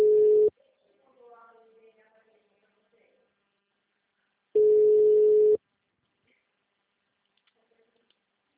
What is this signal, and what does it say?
Ringback tone (Europe, 425 Hz).